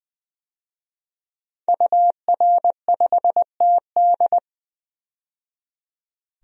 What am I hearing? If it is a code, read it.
Morse code, heard as a keyed tone, about 21 wpm, 700 Hz. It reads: UR5TD
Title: DXCC-100 Call Signs - 5 Characters - 20wpm